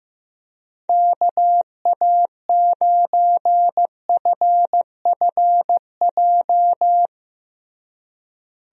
Morse code, heard as a keyed tone, about 16 wpm, 700 Hz. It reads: KA9FFJ
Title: USA Call Signs - 6 Characters - Repeated 3x - 15wpm